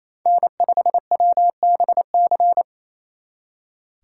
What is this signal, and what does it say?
Morse code, heard as a keyed tone, about 34 wpm, 700 Hz. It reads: N5WBC
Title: Speed Racing - USA 1x3 Call Signs 28wpm